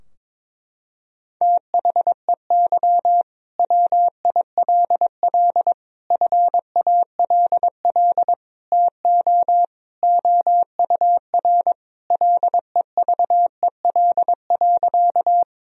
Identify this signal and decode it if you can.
Morse code, heard as a keyed tone, about 22 wpm, 700 Hz. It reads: THEY WILL FALL TO OUR LEVEL.
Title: Sentences from Top 1000 Words - Review Part 4 - 22wpm